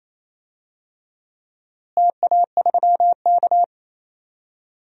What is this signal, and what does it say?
Morse code, heard as a keyed tone, about 28 wpm, 700 Hz. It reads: TA3K
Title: DXCC-100 Call Signs - 4 Characters - Repeated 3x - 28wpm